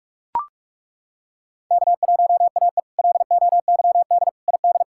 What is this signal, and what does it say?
Morse code, heard as a keyed tone, about 41 wpm, 700 Hz. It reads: K1AE LOYD ID
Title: K1USN SST QSO Practice - 45wpm